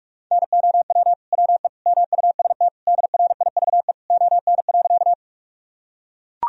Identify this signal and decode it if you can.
Morse code, heard as a keyed tone, about 44 wpm, 700 Hz. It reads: NOW WE MUST DRIVE ON.
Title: Sentences from Top 1000 Words - Review Part 2 - 45wpm